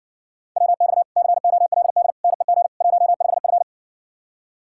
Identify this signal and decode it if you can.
Morse code, heard as a keyed tone, about 60 wpm, 700 Hz. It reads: WX COLD NEG 15F